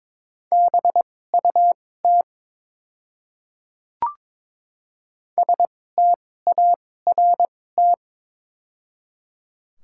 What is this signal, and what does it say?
Morse code, heard as a keyed tone, about 22 wpm, spaced as for 16 wpm, 700 Hz. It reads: BUT START
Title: 2x Character Spacing - Top 500 Words - 22wpm